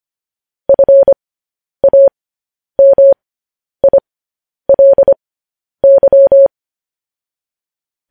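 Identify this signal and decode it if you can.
Morse code, heard as a keyed tone, about 25 wpm, spaced as for 10 wpm, 550 Hz. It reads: FAMILY
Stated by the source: Morse code news headlines, likely sent at 10 WPM.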